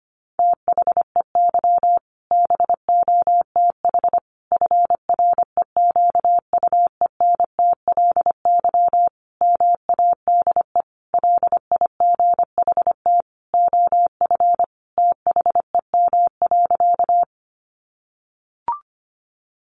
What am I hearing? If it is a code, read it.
Morse code, heard as a keyed tone, about 25 wpm, 700 Hz. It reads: THEY BOTH FREQUENTLY MADE LIGHT OF THEM.
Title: Sentences from Top 1300 Words 25wpm